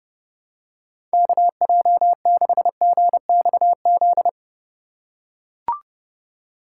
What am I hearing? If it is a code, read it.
Morse code, heard as a keyed tone, about 30 wpm, 700 Hz. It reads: KJ6GXZ
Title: USA Call Signs - 6 Characters - 2x Character Spacing - 30wpm